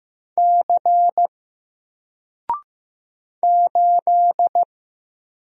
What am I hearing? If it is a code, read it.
Morse code, heard as a keyed tone, about 15 wpm, 700 Hz. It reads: C 8
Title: Single Letter-Number 15wpm